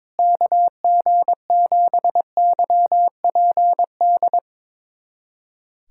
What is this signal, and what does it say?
Morse code, heard as a keyed tone, about 22 wpm, 700 Hz. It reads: KG7YPD
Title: USA Callsign DE Callsign 22wpm